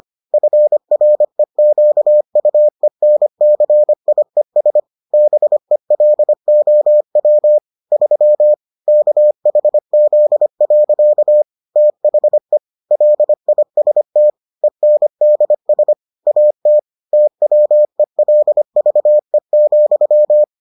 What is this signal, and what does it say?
Morse code, heard as a keyed tone, about 25 wpm, 600 Hz. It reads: FREQUENCIES BELOW 3 KHZ. THE LIST ENDS AT TWELVE,